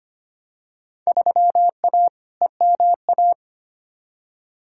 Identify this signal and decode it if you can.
Morse code, heard as a keyed tone, about 25 wpm, 700 Hz. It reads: 3A EMA
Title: Field Day Exchanges - 25wpm